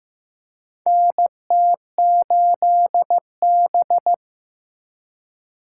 Morse code, heard as a keyed tone, about 15 wpm, 700 Hz. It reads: NT8B
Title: USA Call Signs - 4 Characters - 4x Character Spacing - 15wpm